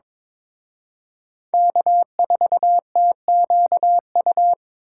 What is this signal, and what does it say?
Morse code, heard as a keyed tone, about 26 wpm, 700 Hz. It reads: K4TQU
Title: Speed Racing - USA 1x3 Call Signs 22wpm